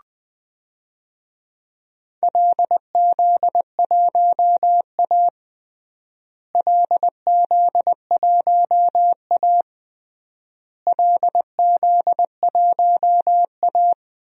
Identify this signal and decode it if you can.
Morse code, heard as a keyed tone, about 20 wpm, 700 Hz. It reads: LZ1A LZ1A LZ1A
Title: DXCC-100 Call Signs - 4 Characters - Repeated 3x - 20wpm